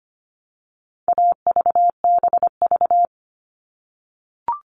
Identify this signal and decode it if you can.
Morse code, heard as a keyed tone, about 25 wpm, 700 Hz. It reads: AVBV
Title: All Cut Numbers - 4-Digit 25wpm